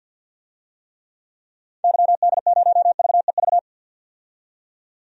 Morse code, heard as a keyed tone, about 53 wpm, 700 Hz. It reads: YD0VEV
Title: DXCC-100 Call Signs - 6 Characters - 50wpm